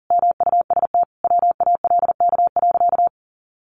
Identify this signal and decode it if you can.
Morse code, heard as a keyed tone, about 40 wpm, 700 Hz. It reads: MUST WALK.
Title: Sentences from Top 300 Words 40wpm